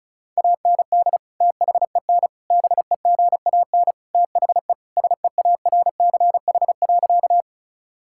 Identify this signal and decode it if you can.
Morse code, heard as a keyed tone, about 34 wpm, 700 Hz. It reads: AND THEN BEGAN THE SEARCH.
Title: Sentences from Top 1000 Words - Review Part 5 - 35wpm